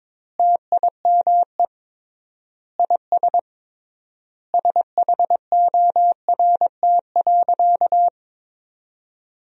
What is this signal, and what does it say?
Morse code, heard as a keyed tone, about 22 wpm, 700 Hz. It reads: TIME IS SHORT.
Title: 3x Word-Spacing - Sentences from Top 500 Words - 22wpm